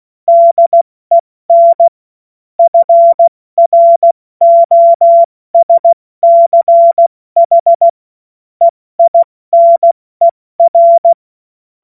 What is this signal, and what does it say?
Morse code, heard as a keyed tone, about 16 wpm, 680 Hz. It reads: DEN FROSCH EINER
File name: Morsekurs / Witze / Witz 08. November 2022 70 F80